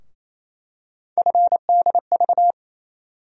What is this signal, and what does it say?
Morse code, heard as a keyed tone, about 28 wpm, 700 Hz. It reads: FDV
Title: USA Call Sign Postfixes - 3 Letters 28wpm